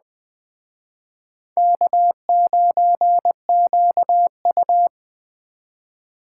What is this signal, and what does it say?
Morse code, heard as a keyed tone, about 20 wpm, 700 Hz. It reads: K9QU